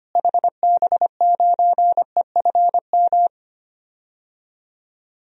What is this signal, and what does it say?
Morse code, heard as a keyed tone, about 25 wpm, 700 Hz. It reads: HB9EFM
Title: DXCC-100 Call Signs - 6 Characters - Repeated 2x - 25wpm